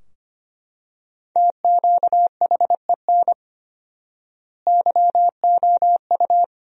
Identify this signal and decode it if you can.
Morse code, heard as a keyed tone, about 25 wpm, 700 Hz. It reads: TQHEN YOU
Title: 4x Word-Spacing - Sentences from Top 100 Words - 25wpm